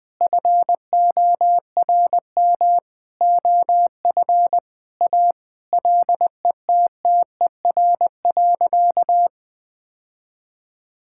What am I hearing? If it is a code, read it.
Morse code, heard as a keyed tone, about 21 wpm, 700 Hz. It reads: FORM OF A LETTER.
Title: Sentences with 5-Word Phrases - Spoken Beginning - 20wpm